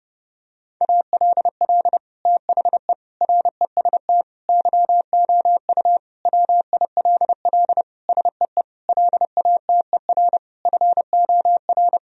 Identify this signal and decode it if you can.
Morse code, heard as a keyed tone, about 30 wpm, 700 Hz. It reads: ALL THE REST YOU WILL SEE LATER FOR
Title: Sentences from Top 1000 Words - Review Part 4 - 30wpm